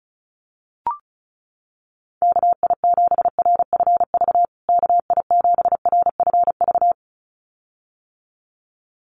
Morse code, heard as a keyed tone, about 34 wpm, 700 Hz. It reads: KI7RFV KI7RFV
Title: USA Call Signs Repeated 2x 35wpm